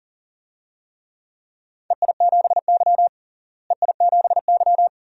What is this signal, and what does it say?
Morse code, heard as a keyed tone, about 40 wpm, 700 Hz. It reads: EI7Y EI7Y
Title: DXCC-100 Call Signs - 4 Characters - Repeated 2x - 40wpm